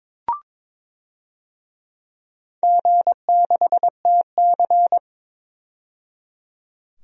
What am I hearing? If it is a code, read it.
Morse code, heard as a keyed tone, about 21 wpm, 700 Hz. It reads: G6TC
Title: DXCC-100 Call Signs - 4 Characters - 22wpm